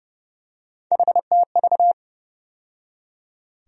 Morse code, heard as a keyed tone, about 32 wpm, 700 Hz. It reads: HTV
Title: USA Call Sign Postfixes - 3 Letters 30wpm